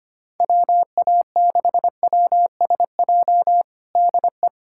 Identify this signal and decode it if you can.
Morse code, heard as a keyed tone, about 25 wpm, 700 Hz. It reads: WA6WSJ DE
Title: USA Callsign DE Callsign - 6 Characters - 25wpm